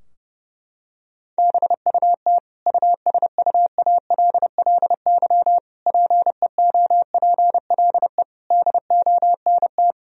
Morse code, heard as a keyed tone, about 30 wpm, 700 Hz. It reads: BUT USUALLY PEOPLE DONT
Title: Sentences from Top 1000 Words - Review Part 3 - 30wpm